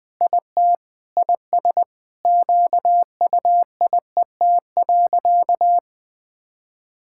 Morse code, heard as a keyed tone, about 20 wpm, 700 Hz. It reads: IT IS QUIET.
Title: Sentences with 3-Word Phrases - Spoken Beginning - 20wpm